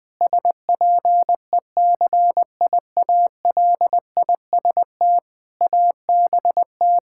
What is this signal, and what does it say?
Morse code, heard as a keyed tone, about 20 wpm, 700 Hz. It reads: SPECIALIST ABT
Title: Practice QSO - Exchange 3 - More Info - 20wpm